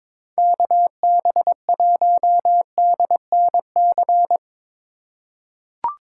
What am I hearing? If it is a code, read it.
Morse code, heard as a keyed tone, about 22 wpm, 700 Hz. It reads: KB1DNC
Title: USA Call Signs - 2x Character Spacing - 22wpm